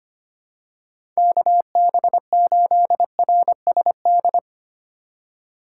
Morse code, heard as a keyed tone, about 25 wpm, 700 Hz. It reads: KB8RSD
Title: USA Call Signs - 6 Characters - 4x Character Spacing - 25wpm